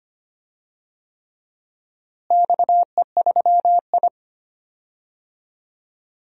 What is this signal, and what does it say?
Morse code, heard as a keyed tone, about 25 wpm, 700 Hz. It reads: XE3I
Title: DXCC-100 Call Signs - 4 Characters - Repeated 3x - 25wpm